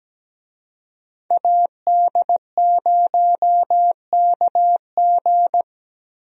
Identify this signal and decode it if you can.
Morse code, heard as a keyed tone, about 17 wpm, 700 Hz. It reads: AD0KG
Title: USA Call Signs - 5 Characters - Repeated 3x - 17wpm